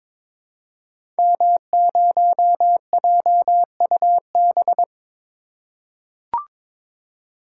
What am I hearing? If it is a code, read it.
Morse code, heard as a keyed tone, about 22 wpm, 700 Hz. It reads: M0JUB
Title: Code Groups - 5 Letter:Numbers - 22wpm